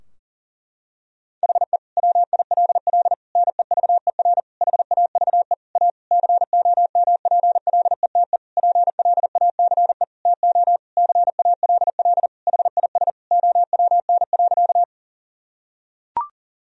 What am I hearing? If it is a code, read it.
Morse code, heard as a keyed tone, about 40 wpm, 700 Hz. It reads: HE WILL NEVER HAVE A COMPLETE PLACE TO CALL HIS OWN.
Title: Sentences from Top 1000 Words - Review Part 2 - 40wpm